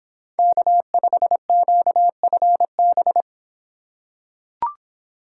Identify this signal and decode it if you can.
Morse code, heard as a keyed tone, about 26 wpm, 700 Hz. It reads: K5QFB